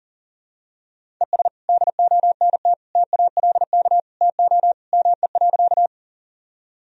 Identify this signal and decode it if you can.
Morse code, heard as a keyed tone, about 40 wpm, 700 Hz. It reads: ES DONT TALK TO ME.
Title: Sentences from Top 400 Words and Abbreviations 40wpm